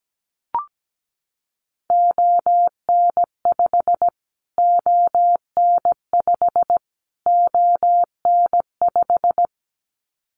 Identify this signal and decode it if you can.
Morse code, heard as a keyed tone, about 17 wpm, 700 Hz. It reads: ON5 ON5 ON5
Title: Callsign Prefixes for Top 25 DXCC - Repeated 3x - 17wpm